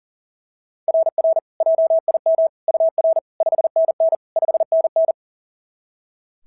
Morse code, heard as a keyed tone, about 38 wpm, 660 Hz. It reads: RR JIM UR 5NN 5NN
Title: SOTA QSO Practice 40wpm